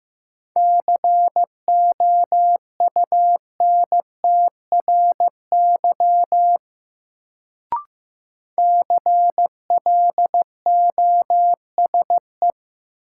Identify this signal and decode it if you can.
Morse code, heard as a keyed tone, about 15 wpm, 700 Hz. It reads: COUNTRY CLOSE
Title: Sentences from Top 300 Words - New Word Drill - 15wpm